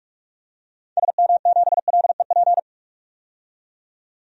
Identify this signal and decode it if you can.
Morse code, heard as a keyed tone, about 45 wpm, 700 Hz. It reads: SM7LEP
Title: DXCC-100 Call Signs - 6 Characters - 45wpm